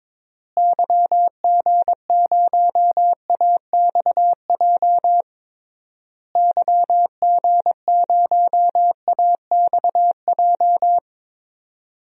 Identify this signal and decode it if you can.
Morse code, heard as a keyed tone, about 22 wpm, 700 Hz. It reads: YG0AXJ YG0AXJ
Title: DXCC-100 Call Signs - 6 Characters - Repeated 3x - 22wpm